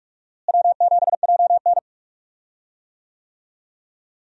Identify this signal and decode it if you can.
Morse code, heard as a keyed tone, about 46 wpm, 700 Hz. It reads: W7JN
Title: USA Call Signs - 4 Characters - Repeated 3x - 45wpm